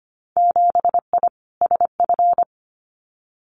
Morse code, heard as a keyed tone, about 25 wpm, 700 Hz. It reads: 7I SF
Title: Winter Field Day Exchanges Encore 25wpm